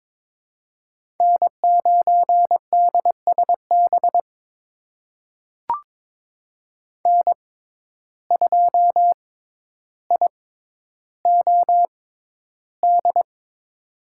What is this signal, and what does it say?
Morse code, heard as a keyed tone, about 22 wpm, spaced as for 11 wpm, 700 Hz. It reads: N9DSB N2IOD
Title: USA Call Signs - 5 Characters - 6x Character Spacing - 22wpm